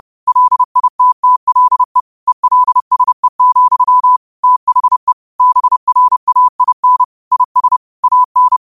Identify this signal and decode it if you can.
Morse code, heard as a keyed tone, about 30 wpm, 1000 Hz. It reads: LITTLE ELSE, THE BRAIN IS AN
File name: qotdfiles / 30wpm